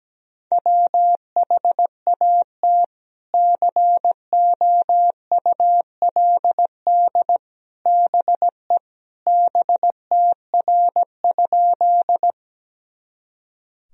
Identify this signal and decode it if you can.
Morse code, heard as a keyed tone, about 17 wpm, 700 Hz. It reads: WHAT COULD BE BTR?
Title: Sentences from Top 400 Words and Abbreviations 17wpm